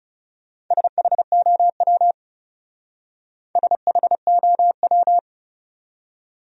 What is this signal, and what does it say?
Morse code, heard as a keyed tone, about 37 wpm, 700 Hz. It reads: SHOW SHOW
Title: Speed-Racing Top 200 Words 30wpm